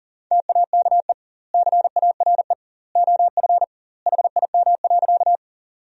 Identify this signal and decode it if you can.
Morse code, heard as a keyed tone, about 40 wpm, 700 Hz. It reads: TAKE CARE OF HIM.
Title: 2x Word-Spacing - Sentences from Top 300 Words - 40wpm